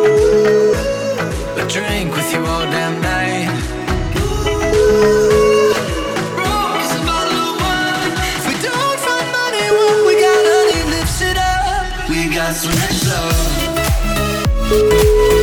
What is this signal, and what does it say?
Ringback tone (Europe, 425 Hz), with noise.